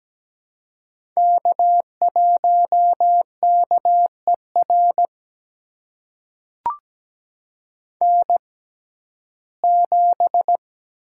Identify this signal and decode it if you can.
Morse code, heard as a keyed tone, about 17 wpm, spaced as for 12 wpm, 700 Hz. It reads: K1KER N7
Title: USA Call Signs - 5 Characters - 6x Character Spacing - 17wpm